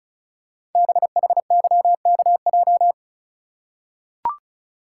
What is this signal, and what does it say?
Morse code, heard as a keyed tone, about 36 wpm, 700 Hz. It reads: BHYKJ
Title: Code Groups - 5 Letters - 35wpm